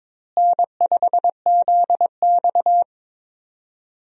Morse code, heard as a keyed tone, about 22 wpm, 700 Hz. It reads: N5ZX
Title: USA Call Signs - 4 Characters - Repeated 3x - 22wpm